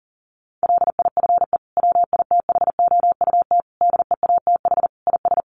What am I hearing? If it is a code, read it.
Morse code, heard as a keyed tone, about 40 wpm, 700 Hz. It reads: LIFE WITHOUT DEATH IS
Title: Sentences from Top 1100 Words 40wpm